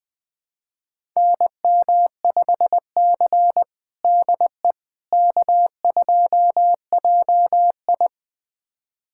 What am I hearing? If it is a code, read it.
Morse code, heard as a keyed tone, about 20 wpm, 700 Hz. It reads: NM5C DE K2JI
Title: USA Callsign DE Callsign - 4 Characters - 20wpm